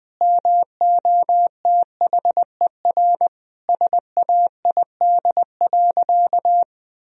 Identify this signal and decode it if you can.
Morse code, heard as a keyed tone, about 21 wpm, 700 Hz. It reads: MOTHER SAID.
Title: Sentences from Top 1000 Words - Review Part 3 - 20wpm